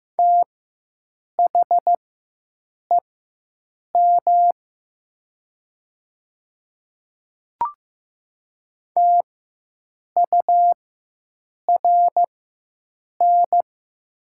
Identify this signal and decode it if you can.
Morse code, heard as a keyed tone, about 15 wpm, spaced as for 7 wpm, 700 Hz. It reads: THEM TURN
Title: 4x Character Spacing - Top 300 Words - 15wpm